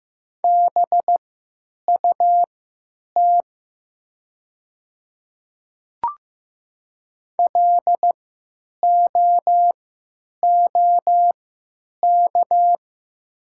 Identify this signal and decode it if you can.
Morse code, heard as a keyed tone, about 15 wpm, spaced as for 8 wpm, 700 Hz. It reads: BUT LOOK
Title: 3x Character Spacing - Top 200 Words - 15wpm